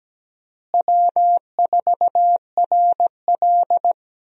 Morse code, heard as a keyed tone, about 17 wpm, 700 Hz. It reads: W4RL
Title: USA Call Signs - 4 Characters - Repeated 2x - 17wpm